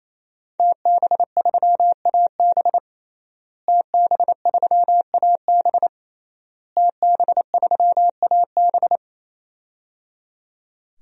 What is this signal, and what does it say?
Morse code, heard as a keyed tone, about 28 wpm, 700 Hz. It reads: TB3AB TB3AB TB3AB
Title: DXCC-100 Call Signs - 5 Characters - Repeated 3x - 28wpm